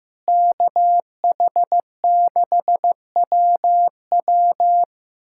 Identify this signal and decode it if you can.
Morse code, heard as a keyed tone, about 16 wpm, 700 Hz. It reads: KH6WW
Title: USA Callsign DE Callsign - 5 Characters - 15wpm